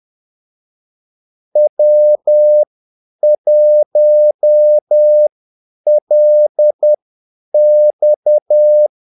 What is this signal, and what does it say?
Morse code, heard as a keyed tone, about 10 wpm, spaced as for 8 wpm, 600 Hz. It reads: W1LX
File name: Callsigns-10-8wpm